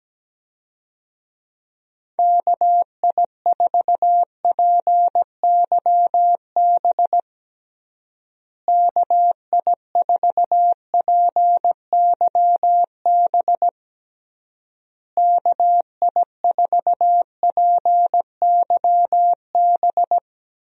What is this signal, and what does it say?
Morse code, heard as a keyed tone, about 17 wpm, 700 Hz. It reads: KI4PYB KI4PYB KI4PYB
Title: USA Call Signs - 6 Characters - Repeated 3x - 17wpm